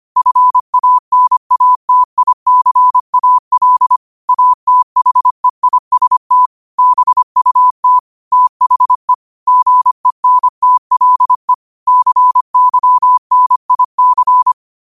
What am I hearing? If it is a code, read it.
Morse code, heard as a keyed tone, about 25 wpm, 1000 Hz. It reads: FANATICAL ATHEIST BUT THE GENTLE CYNIC